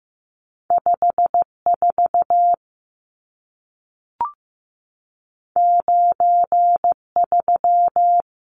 Morse code, heard as a keyed tone, about 15 wpm, 700 Hz. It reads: 54 93
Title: Numbers - 2-Digit 15wpm